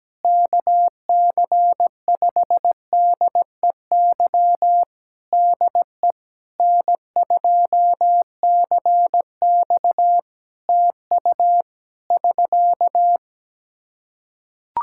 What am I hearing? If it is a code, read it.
Morse code, heard as a keyed tone, about 17 wpm, 700 Hz. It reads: KC5DEY DE N2CX TU <SK>
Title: Practice QSO - Exchange 4 - Ending - 17wpm